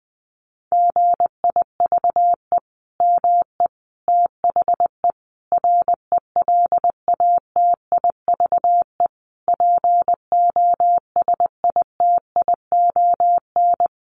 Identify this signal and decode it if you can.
Morse code, heard as a keyed tone, about 20 wpm, 700 Hz. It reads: GIVE ME THE RELATIVE POSITION
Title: Sentences from Top 1400 Words 20wpm